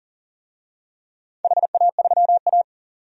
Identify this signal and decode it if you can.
Morse code, heard as a keyed tone, about 38 wpm, 700 Hz. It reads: HA3A
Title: DXCC-100 Call Signs - 4 Characters - 40wpm